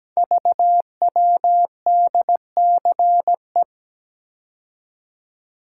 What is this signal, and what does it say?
Morse code, heard as a keyed tone, about 17 wpm, 700 Hz. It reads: VWDCE
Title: Code Groups - 5 Letters - 17wpm